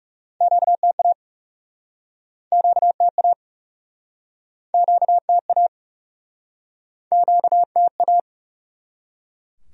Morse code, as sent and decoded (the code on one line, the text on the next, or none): --.- - .- / --.- - .- / --.- - .- / --.- - .-
QTA QTA QTA QTA